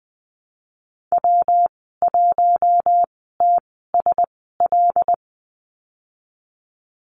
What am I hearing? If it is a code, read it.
Morse code, heard as a keyed tone, about 20 wpm, spaced as for 15 wpm, 700 Hz. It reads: W1TSL
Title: 128 Lesson 21g - Number 6 - Call Signs 20wpm